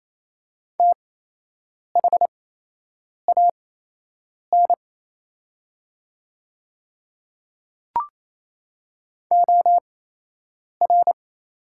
Morse code, heard as a keyed tone, about 28 wpm, spaced as for 8 wpm, 700 Hz. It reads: THAN OR
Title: 8x Character Spacing - Top 100 Words - 28wpm